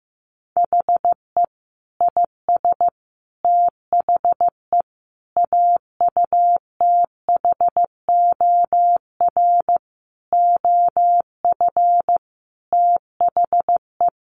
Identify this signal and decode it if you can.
Morse code, heard as a keyed tone, about 15 wpm, 700 Hz. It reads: HE IS THE AUTHOR OF THE
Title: Sentences from Top 1200 Words 15wpm